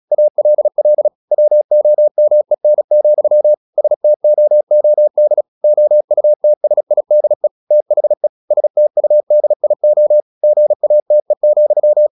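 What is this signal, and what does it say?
Morse code, heard as a keyed tone, about 36 wpm, 600 Hz. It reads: ALL WOMEN, STOOD OUTSIDE THE STUDIO GATE,
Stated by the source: Morse practice at 36 wpm, text of Ken Follett's novel Never, from CW med Gnister.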